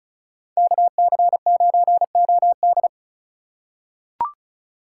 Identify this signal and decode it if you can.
Morse code, heard as a keyed tone, about 36 wpm, 700 Hz. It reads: KC9OD
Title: USA Call Signs - 5 Characters - 2x Character Spacing - 35wpm